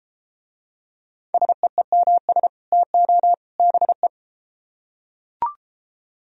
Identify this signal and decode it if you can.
Morse code, heard as a keyed tone, about 32 wpm, 700 Hz. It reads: SEEMS TO BE